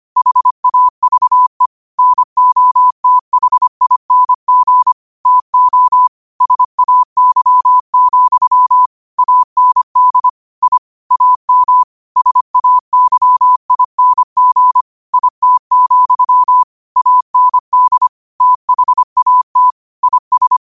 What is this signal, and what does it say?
Morse code, heard as a keyed tone, about 25 wpm, 1000 Hz. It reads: HAVE NOTHING TO SAY, AND I AM SAYING IT, AND THAT IS